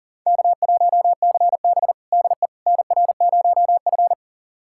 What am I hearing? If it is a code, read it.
Morse code, heard as a keyed tone, about 40 wpm, 700 Hz. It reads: K1CB DE NR0F
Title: USA Callsign DE Callsign - 4 Characters - 40wpm